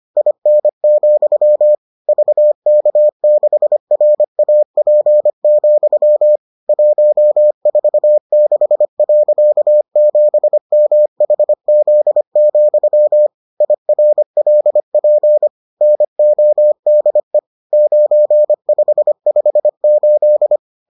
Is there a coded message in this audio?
Yes: Morse code from 0.2 to 20.6 s, heard as a keyed tone, about 25 wpm, 600 Hz. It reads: IN, VK6RAP, 146.7MHZ, IRLP NODE 9558